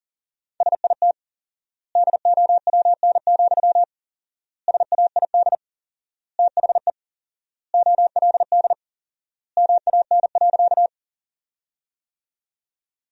Morse code, heard as a keyed tone, about 40 wpm, 700 Hz. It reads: SIT DOWN, SAID THE OLD MAN.
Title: 4x Word-Spacing - Sentences from Top 500 Words - 40wpm